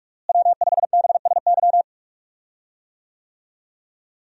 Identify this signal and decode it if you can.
Morse code, heard as a keyed tone, about 42 wpm, 700 Hz. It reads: W5BSY
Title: USA Call Signs - 5 Characters - Repeated 2x - 45wpm